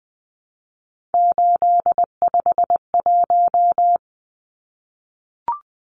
Morse code, heard as a keyed tone, about 20 wpm, 700 Hz. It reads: 851
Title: Numbers - 3-Digit 20wpm